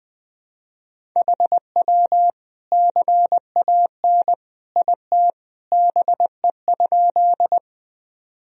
Morse code, heard as a keyed tone, about 21 wpm, 700 Hz. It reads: HW CAN IT BE?
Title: Sentences from Top 100 Words and Abbreviations 20wpm